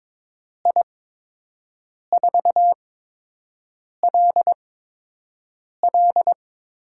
Morse code, heard as a keyed tone, about 22 wpm, spaced as for 6 wpm, 700 Hz. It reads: I4LL
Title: USA Call Signs - 6 Characters - 8x Character Spacing - 22wpm